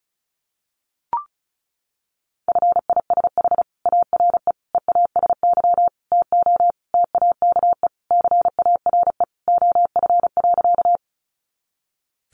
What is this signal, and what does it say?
Morse code, heard as a keyed tone, about 34 wpm, 700 Hz. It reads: FISH ARE EASY TO TAKE CARE OF.
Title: Additional Easy Phrases 35wpm